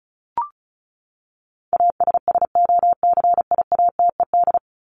Morse code, heard as a keyed tone, about 34 wpm, 700 Hz. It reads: ASSOCIATED
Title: ED Suffix - Top 100 Words with Base Words 35wpm